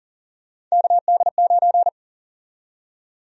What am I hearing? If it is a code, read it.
Morse code, heard as a keyed tone, about 40 wpm, 700 Hz. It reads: KD9
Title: USA Call Sign Prefixes - 40wpm